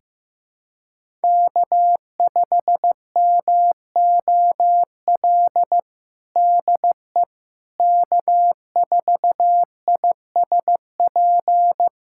Morse code, heard as a keyed tone, about 16 wpm, 700 Hz. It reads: K5MOL DE K4ISP
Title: USA Callsign DE Callsign - 5 Characters - 15wpm